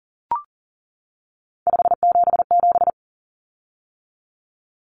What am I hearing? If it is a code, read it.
Morse code, heard as a keyed tone, about 40 wpm, 700 Hz. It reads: 577
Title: Common Cut Numbers - 3-Digit 40wpm